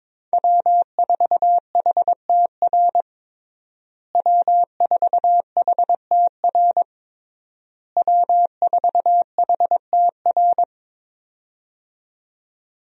Morse code, heard as a keyed tone, about 21 wpm, 700 Hz. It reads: W4HTR W4HTR W4HTR
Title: USA Call Signs - 5 Characters - Repeated 3x - 22wpm